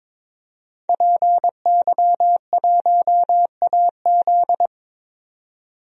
Morse code, heard as a keyed tone, about 21 wpm, 700 Hz. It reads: PY1AZ
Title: DXCC-100 Call Signs - 5 Characters - 22wpm